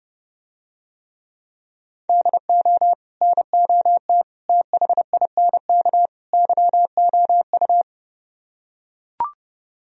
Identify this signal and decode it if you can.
Morse code, heard as a keyed tone, about 30 wpm, 700 Hz. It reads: DO NOT THINK YOU